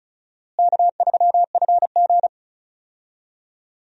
Morse code, heard as a keyed tone, about 36 wpm, 700 Hz. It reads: K3FG
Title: USA Call Signs - 4 Characters - Repeated 2x - 35wpm